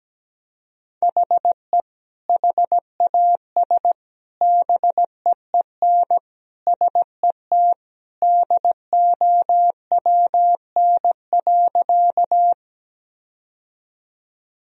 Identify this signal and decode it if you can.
Morse code, heard as a keyed tone, about 17 wpm, 700 Hz. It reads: HE HAS BEEN SET DOWN.
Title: Sentences from Top 1000 Words - Review Part 1 - 17wpm